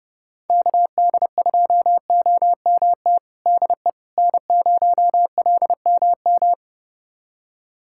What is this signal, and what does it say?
Morse code, heard as a keyed tone, about 30 wpm, 700 Hz. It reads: KD2OMT DE N0LMM
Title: USA Callsign DE Callsign 30wpm